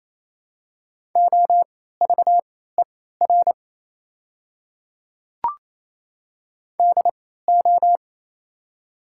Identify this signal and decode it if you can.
Morse code, heard as a keyed tone, about 27 wpm, spaced as for 16 wpm, 700 Hz. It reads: OVER DO
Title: 3x Character Spacing - Top 100 Words - 28wpm